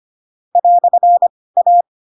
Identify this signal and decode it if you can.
Morse code, heard as a keyed tone, about 25 wpm, spaced as for 18 wpm, 700 Hz. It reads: "A
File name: Pinocchio / Pinocchio ch1 12 25 ik1hge